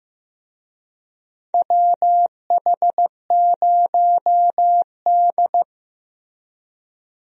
Morse code, heard as a keyed tone, about 15 wpm, 700 Hz. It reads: WH0D
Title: USA Call Signs - 4 Characters - 15wpm